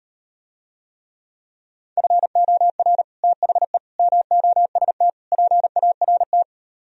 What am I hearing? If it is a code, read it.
Morse code, heard as a keyed tone, about 38 wpm, 700 Hz. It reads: FOR THE MOST PART